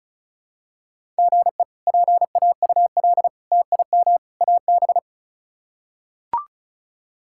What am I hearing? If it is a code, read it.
Morse code, heard as a keyed tone, about 36 wpm, 700 Hz. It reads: GE PAUL TIM AB
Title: K1USN SST QSO Practice - 35wpm